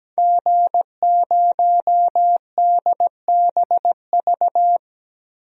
Morse code, heard as a keyed tone, about 17 wpm, 700 Hz. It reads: G0DBV
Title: DXCC-100 Call Signs - 5 Characters - Repeated 2x - 17wpm